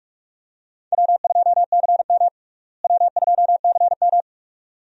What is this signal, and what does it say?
Morse code, heard as a keyed tone, about 46 wpm, 700 Hz. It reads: W2CM W2CM
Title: USA Call Signs - 4 Characters - Repeated 2x - 45wpm